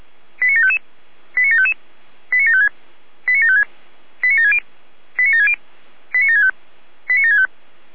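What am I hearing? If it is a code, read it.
Five-tone selective call (ZVEI), heard as quick runs of five tones, tones of about 70 ms. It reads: 89755 89755 89756 89756 89760 89760 89765 89765
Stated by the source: ZVEI five-tone selective call received on an SDR.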